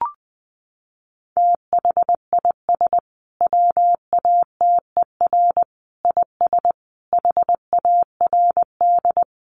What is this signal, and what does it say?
Morse code, heard as a keyed tone, about 20 wpm, 700 Hz. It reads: THIS WATER IS HARD
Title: Sentences from Top 1300 Words 20wpm